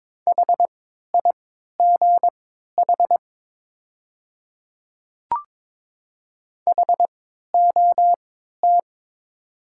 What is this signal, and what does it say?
Morse code, heard as a keyed tone, about 22 wpm, spaced as for 13 wpm, 690 Hz. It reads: HIGH HOT
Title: 3x Character Spacing - Top 200 Words - 22wpm